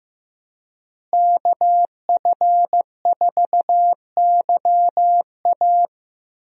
Morse code, heard as a keyed tone, about 16 wpm, 700 Hz. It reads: KF4YA
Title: USA Call Signs - 5 Characters - Repeated 3x - 15wpm